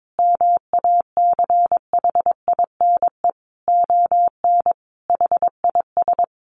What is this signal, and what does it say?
Morse code, heard as a keyed tone, about 22 wpm, 700 Hz. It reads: MACHINE ON HIS
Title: A Sound Of Thunder 22wpm